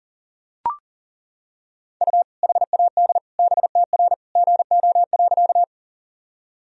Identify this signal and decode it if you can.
Morse code, heard as a keyed tone, about 40 wpm, 700 Hz. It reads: U HAD BTR GO.
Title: Sentences from Top 400 Words and Abbreviations 40wpm